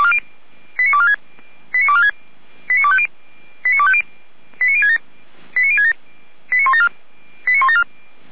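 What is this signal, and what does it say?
Five-tone selective call (ZVEI), heard as quick runs of five tones, tones of about 70 ms. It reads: 89246 89246 89250 89067 89067 89174 89174
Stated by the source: ZVEI five-tone selective call received on an SDR, served by feuerwehr-orlamuende (Orlamünde fire brigade).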